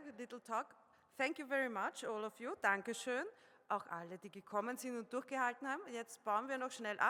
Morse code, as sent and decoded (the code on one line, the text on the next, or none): none